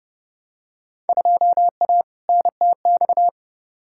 Morse code, heard as a keyed tone, about 30 wpm, 700 Hz. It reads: 2A NTX
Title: Field Day Exchanges - 30wpm